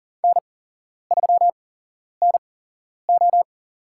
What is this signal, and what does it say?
Morse code, heard as a keyed tone, about 42 wpm, spaced as for 11 wpm, 700 Hz. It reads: N3NO
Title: USA Call Signs - 4 Characters - 8x Character Spacing - 40wpm